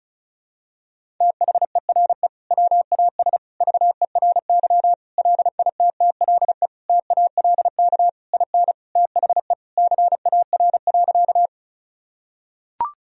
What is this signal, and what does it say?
Morse code, heard as a keyed tone, about 34 wpm, 700 Hz. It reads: THERE WAS VERY LITTLE TALK IN THE CAR.
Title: Sentences from Top 1000 Words - Review Part 1 - 35wpm